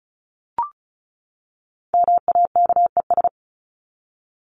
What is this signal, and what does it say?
Morse code, heard as a keyed tone, about 34 wpm, 700 Hz. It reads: MAKES
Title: ES:S Suffix - Top 100 Words with Base Words 35wpm